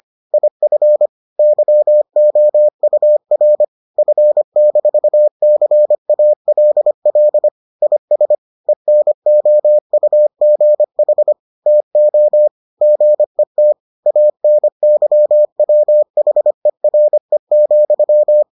Morse code, heard as a keyed tone, about 25 wpm, 600 Hz. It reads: IF YOUR F-CALL IS ENOUGH TO GET ANYWHERE,